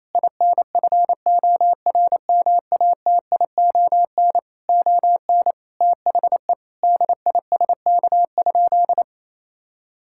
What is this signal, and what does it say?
Morse code, heard as a keyed tone, about 28 wpm, 700 Hz. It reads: INFORMATION ON THE DISK?
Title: Sentences from Top 1000 Words - Review Part 6 - 28wpm